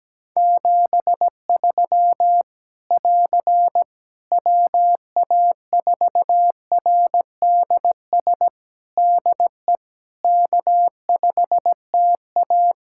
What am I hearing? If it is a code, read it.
Morse code, heard as a keyed tone, about 17 wpm, 700 Hz. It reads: 73 <AR> WA4RDS DE K5TA
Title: Practice QSO - Exchange 4 - Ending - 17wpm